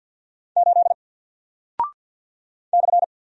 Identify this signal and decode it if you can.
Morse code, heard as a keyed tone, about 49 wpm, 700 Hz. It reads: 8 /
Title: Single Character (Letter:Number:Punctuation:Prosign) 50wpm